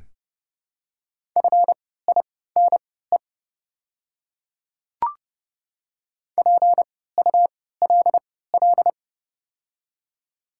Morse code, heard as a keyed tone, about 30 wpm, spaced as for 17 wpm, 700 Hz. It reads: FINE PULL
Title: 3x Character Spacing - Top 500 Words - 30wpm